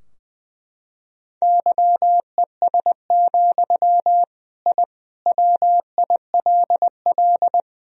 Morse code, heard as a keyed tone, about 21 wpm, 700 Hz. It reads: YES, I WILL
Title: Sentences from Top 1000 Words - Review Part 2 - 20wpm